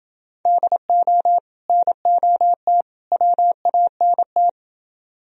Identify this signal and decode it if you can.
Morse code, heard as a keyed tone, about 30 wpm, 700 Hz. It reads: DO NOT WANT